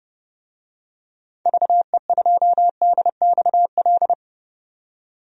Morse code, heard as a keyed tone, about 31 wpm, 700 Hz. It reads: VE2DXL
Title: DXCC-100 Call Signs - 6 Characters - 30wpm